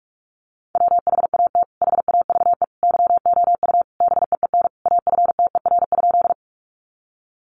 Morse code, heard as a keyed tone, about 44 wpm, 700 Hz. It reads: WHAT HAVE YOU BEEN AFTER?
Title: Sentences from Top 200 Words 45wpm